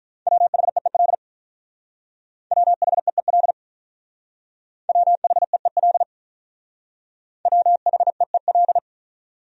Morse code, heard as a keyed tone, about 44 wpm, 700 Hz. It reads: WHEEL WHEEL WHEEL WHEEL
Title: Speed-Racing Top 500 Words 35wpm